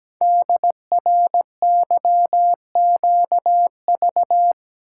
Morse code, heard as a keyed tone, about 17 wpm, 700 Hz. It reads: DRYQV
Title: Code Groups - 5 Letters - 17wpm